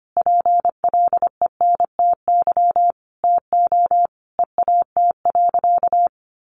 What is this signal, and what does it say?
Morse code, heard as a keyed tone, about 25 wpm, 700 Hz. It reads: PLENTY TO EAT.
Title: Sentences from Top 1300 Words 25wpm